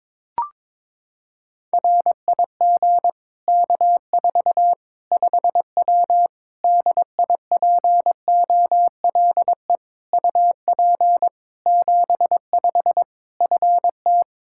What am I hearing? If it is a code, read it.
Morse code, heard as a keyed tone, about 22 wpm, 700 Hz. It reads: RIG K4 5W DIPOLE UP 75 FT
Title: Practice QSO - Exchange 2 - Info - 22wpm